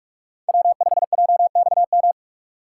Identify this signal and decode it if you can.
Morse code, heard as a keyed tone, about 40 wpm, 700 Hz. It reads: W5JXM